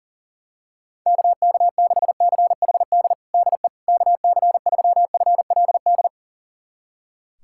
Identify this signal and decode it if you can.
Morse code, heard as a keyed tone, about 40 wpm, 700 Hz. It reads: KK6CHD DE KC3FLD
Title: USA Callsign DE Callsign - 6 Characters - 40wpm